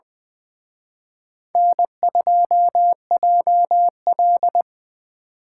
Morse code, heard as a keyed tone, about 24 wpm, 700 Hz. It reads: N2JL
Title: Speed Racing - USA 1x2 Call Signs 20wpm